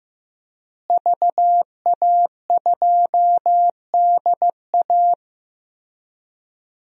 Morse code, heard as a keyed tone, about 15 wpm, 700 Hz. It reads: VA2DA
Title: DXCC-100 Call Signs - 5 Characters - Repeated 3x - 15wpm